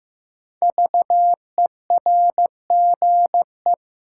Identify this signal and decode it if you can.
Morse code, heard as a keyed tone, about 15 wpm, 700 Hz. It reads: VERGE
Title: Sets of 4 Words - New Word Drill - 15wpm